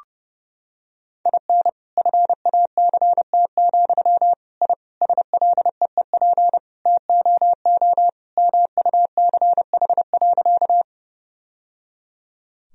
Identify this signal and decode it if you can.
Morse code, heard as a keyed tone, about 32 wpm, 700 Hz. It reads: IN FACT, I SLEEP TOO MUCH.
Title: Sentences from Top 1000 Words - Review Part 3 - 30wpm